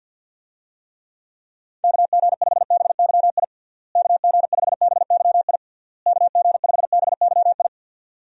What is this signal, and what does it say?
Morse code, heard as a keyed tone, about 51 wpm, 700 Hz. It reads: KG5BYI KG5BYI KG5BYI
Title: USA Call Signs - 6 Characters - Repeated 3x - 50wpm